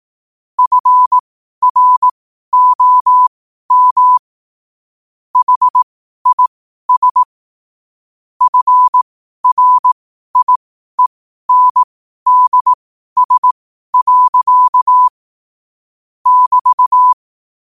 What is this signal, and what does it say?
Morse code, heard as a keyed tone, about 18 wpm, spaced as for 12 wpm, 1000 Hz. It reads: FROM HIS FRIENDS. <BT>
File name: qotdfiles / 12wpm